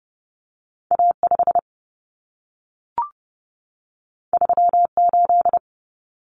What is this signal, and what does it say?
Morse code, heard as a keyed tone, about 30 wpm, 700 Hz. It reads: A5 38
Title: Common Cut Numbers - 2-Digit 30wpm